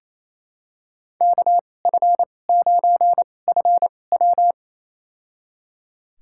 Morse code, heard as a keyed tone, about 28 wpm, spaced as for 20 wpm, 700 Hz. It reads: KF9FW
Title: USA Call Signs - 5 Characters - 2x Character Spacing - 28wpm